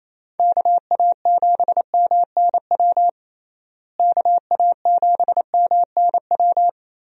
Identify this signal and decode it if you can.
Morse code, heard as a keyed tone, about 28 wpm, 700 Hz. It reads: KA7MNW KA7MNW
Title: USA Call Signs - 6 Characters - Repeated 3x - 28wpm